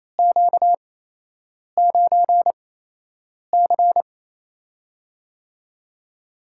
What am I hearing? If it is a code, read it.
Morse code, heard as a keyed tone, about 27 wpm, spaced as for 8 wpm, 700 Hz. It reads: Q9C
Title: USA Call Signs - 4 Characters - 8x Character Spacing - 28wpm